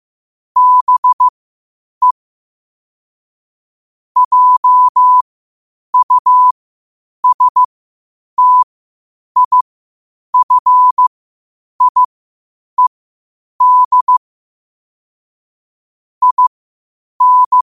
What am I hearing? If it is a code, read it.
Morse code, heard as a keyed tone, about 15 wpm, spaced as for 8 wpm, 1000 Hz. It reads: BE JUSTIFIED IN
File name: qotdfiles / 8wpm